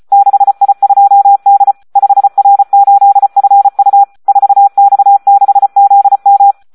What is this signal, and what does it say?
Morse code, heard as a keyed tone, about 34 wpm, 800 Hz. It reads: 6I2D 5R8FU 4X6ZM
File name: Audio practica CW / SD 1 34